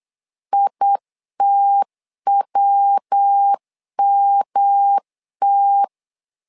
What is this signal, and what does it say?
Morse code, heard as a keyed tone, about 8 wpm, 800 Hz. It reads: ITWMT
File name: LEZIONI / 3 MWRO / 3 MW 8wpm 2501